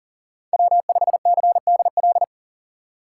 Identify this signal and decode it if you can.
Morse code, heard as a keyed tone, about 40 wpm, 700 Hz. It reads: W5CDL
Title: USA Call Signs - 5 Characters - 2x Character Spacing - 40wpm